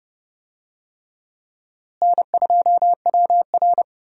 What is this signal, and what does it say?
Morse code, heard as a keyed tone, about 30 wpm, 700 Hz. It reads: N2WR
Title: USA Call Signs - 4 Characters - Repeated 3x - 30wpm